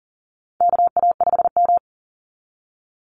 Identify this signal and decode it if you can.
Morse code, heard as a keyed tone, about 37 wpm, 700 Hz. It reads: KA5M
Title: Call Signs Repeated 2x - 40wpm